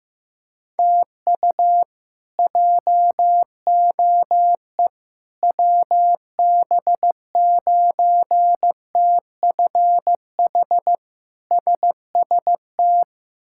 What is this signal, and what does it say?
Morse code, heard as a keyed tone, about 15 wpm, 700 Hz. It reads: TU JOE WB9TFH SST
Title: K1USN SST QSO Practice - 15wpm